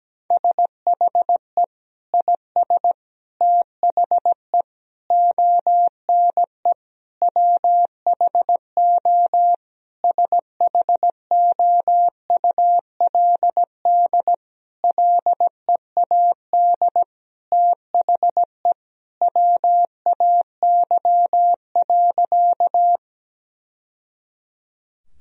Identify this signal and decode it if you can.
Morse code, heard as a keyed tone, about 17 wpm, 700 Hz. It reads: SHE IS THE ONE WHO SHOULD LEAD THE WAY.
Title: Sentences from Top 1000 Words - Review Part 2 - 17wpm